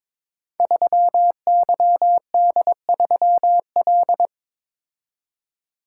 Morse code, heard as a keyed tone, about 22 wpm, 700 Hz. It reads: 3YD3L
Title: Code Groups - 5 Letter:Numbers - 22wpm